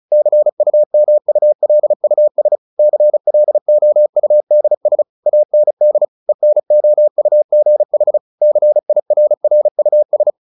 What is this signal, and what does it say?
Morse code, heard as a keyed tone, about 34 wpm, 600 Hz. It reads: CUMULUS CLOUDS AND ENOUGH CIRRUS
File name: the old man and the sea 2 35wpm